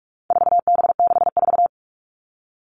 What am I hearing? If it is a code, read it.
Morse code, heard as a keyed tone, about 44 wpm, 700 Hz. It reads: 4B64
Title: All Cut Numbers - 4-Digit 45wpm